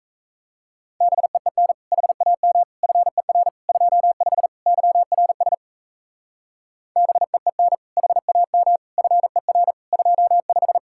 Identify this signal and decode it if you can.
Morse code, heard as a keyed tone, about 36 wpm, 700 Hz. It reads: BEEN HAM FER 25 YRS BEEN HAM FER 25